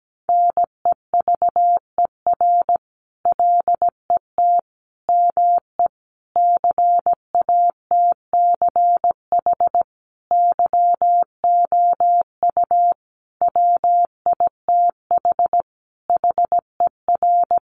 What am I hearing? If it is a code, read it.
Morse code, heard as a keyed tone, about 17 wpm, 700 Hz. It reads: NEVER LET ME CATCH YOU WITH HER
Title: Sentences from Top 1500 Words 17wpm